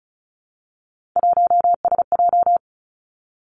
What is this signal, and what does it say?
Morse code, heard as a keyed tone, about 36 wpm, 700 Hz. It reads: 1SJ
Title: Three Challenging Letter-Number Combinations 35wpm